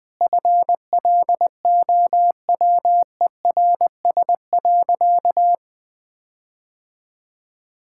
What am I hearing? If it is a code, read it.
Morse code, heard as a keyed tone, about 21 wpm, 700 Hz. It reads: FLOWERS.
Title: Sentences with 9-Word Phrases - 20wpm